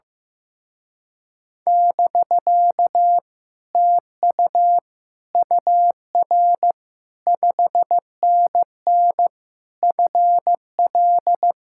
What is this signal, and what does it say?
Morse code, heard as a keyed tone, about 15 wpm, 700 Hz. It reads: <BK> TU UR 5NN FL
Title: Speed-Racing Common QSO Phrases Encore - 15wpm